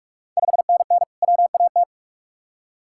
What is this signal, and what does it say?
Morse code, heard as a keyed tone, about 46 wpm, 690 Hz. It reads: 5NN WAT
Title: Ontario QSO Party Exchange 45wpm